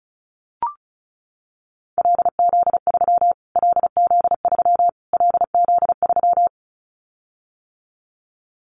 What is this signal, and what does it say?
Morse code, heard as a keyed tone, about 34 wpm, 700 Hz. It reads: LZ3 LZ3 LZ3
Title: Callsign Prefixes for Top 25 DXCC - Repeated 3x - 35wpm